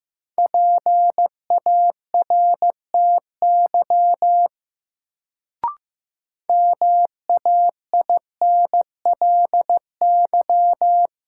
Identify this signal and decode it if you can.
Morse code, heard as a keyed tone, about 15 wpm, 700 Hz. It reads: PARTY MAINLY
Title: Sentences from Top 1400 Words - New Word Drill - 15wpm